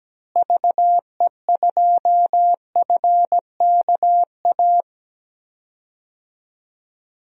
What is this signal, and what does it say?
Morse code, heard as a keyed tone, about 17 wpm, 700 Hz. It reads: VE2FKA
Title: DXCC-100 Call Signs - 6 Characters - Repeated 3x - 17wpm